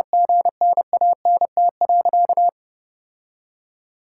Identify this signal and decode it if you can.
Morse code, heard as a keyed tone, about 32 wpm, 700 Hz. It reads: GNANT.
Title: Sentences from Top 1000 Words - Review Part 6 - 30wpm